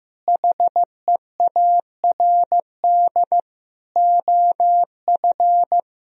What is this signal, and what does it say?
Morse code, heard as a keyed tone, about 15 wpm, 700 Hz. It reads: HEARD OF
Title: Sentences from Top 1000 Words - Review Part 3 - 15wpm